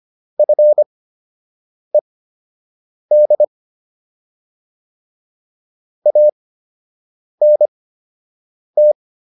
Morse code, heard as a keyed tone, about 25 wpm, spaced as for 7 wpm, 600 Hz. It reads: FED ANT